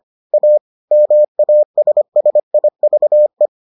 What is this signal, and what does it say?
Morse code, heard as a keyed tone, about 25 wpm, 600 Hz. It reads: A MASSIVE